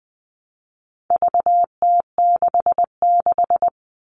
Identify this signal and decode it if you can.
Morse code, heard as a keyed tone, about 20 wpm, 700 Hz. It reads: VT66
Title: All Cut Numbers - 4-Digit 20wpm